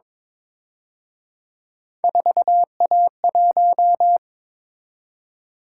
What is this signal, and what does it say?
Morse code, heard as a keyed tone, about 26 wpm, 700 Hz. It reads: 4A1
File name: Speed-Racing - Three Letter-Number Combinations - 22wpm